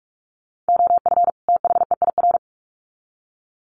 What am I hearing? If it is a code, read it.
Morse code, heard as a keyed tone, about 44 wpm, 700 Hz. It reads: OF THEIR
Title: Sets of 2 Words 45wpm